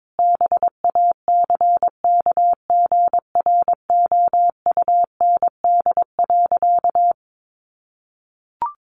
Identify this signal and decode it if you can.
Morse code, heard as a keyed tone, about 22 wpm, 700 Hz. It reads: BACKGROUND.
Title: Sentences from Top 1500 Words 22wpm